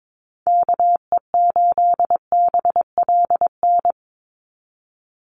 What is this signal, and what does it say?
Morse code, heard as a keyed tone, about 22 wpm, 700 Hz. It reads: KE8BLN
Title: USA Call Signs 22wpm